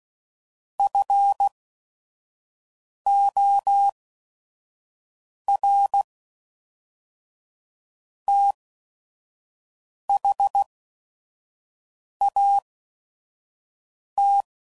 Morse code, heard as a keyed tone, about 16 wpm, spaced as for 5 wpm, 800 Hz. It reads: FOR THAT
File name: CW Code Groups @5WPM / QSOs / QSO WA7QOX 1